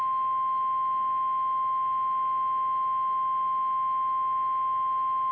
Emergency Alert System (SAME) tones (the 1050 Hz attention tone). It is heard from 0.4 to 4.9 s.
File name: eas / KKPRKXPN / KKPRKXPN